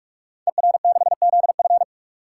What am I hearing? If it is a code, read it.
Morse code, heard as a keyed tone, about 44 wpm, 700 Hz. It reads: ER6ZF
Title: Code Groups - 5 Letter:Numbers - 45wpm